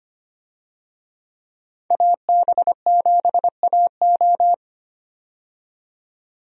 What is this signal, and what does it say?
Morse code, heard as a keyed tone, about 25 wpm, 700 Hz. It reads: AB7AO
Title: USA Call Signs - 5 Characters - 25wpm